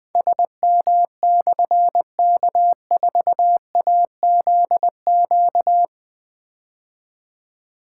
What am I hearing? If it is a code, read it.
Morse code, heard as a keyed tone, about 20 wpm, 700 Hz. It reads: SM/K4AZQ
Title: Difficult Call Signs 20wpm